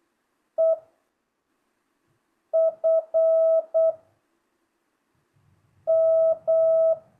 Morse code, heard as a keyed tone, about 8 wpm, 660 Hz. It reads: E F M